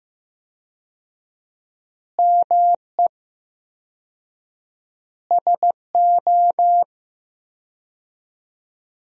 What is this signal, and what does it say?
Morse code, heard as a keyed tone, about 15 wpm, 700 Hz. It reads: ME SO
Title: 4x Word-Spacing - Sentences from Top 400 Words - 15wpm